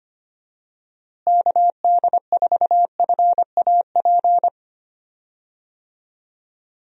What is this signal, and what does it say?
Morse code, heard as a keyed tone, about 25 wpm, 700 Hz. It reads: KD4FAP
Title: USA Call Signs - 6 Characters - Repeated 2x - 25wpm